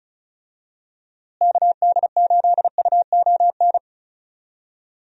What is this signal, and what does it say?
Morse code, heard as a keyed tone, about 34 wpm, 700 Hz. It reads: KD8UON
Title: USA Call Signs - 6 Characters - Repeated 2x - 35wpm